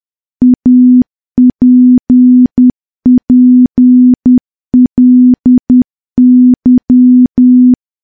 Morse code, heard as a keyed tone, about 10 wpm, 250 Hz. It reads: APPLY
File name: morse / morse-training-10WPM